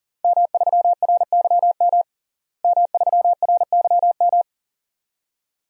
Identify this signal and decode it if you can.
Morse code, heard as a keyed tone, about 37 wpm, 690 Hz. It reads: M3RYM M3RYM
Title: DXCC-100 Call Signs - 5 Characters - Repeated 2x - 40wpm